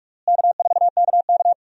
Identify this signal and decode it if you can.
Morse code, heard as a keyed tone, about 43 wpm, 700 Hz. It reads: K4KK
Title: USA Call Signs - 4 Characters - Repeated 3x - 45wpm